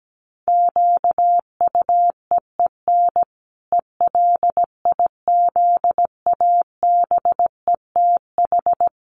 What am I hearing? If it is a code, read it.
Morse code, heard as a keyed tone, about 17 wpm, 700 Hz. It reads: QUEEN ELIZABETH
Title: A Sound Of Thunder 17wpm